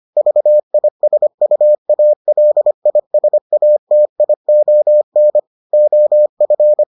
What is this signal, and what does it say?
Morse code, heard as a keyed tone, about 25 wpm, 600 Hz. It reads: VISUALISATION OF